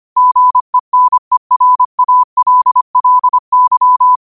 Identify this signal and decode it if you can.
Morse code, heard as a keyed tone, about 25 wpm, 1000 Hz. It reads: GENERALLY
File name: qotdfiles / 25wpm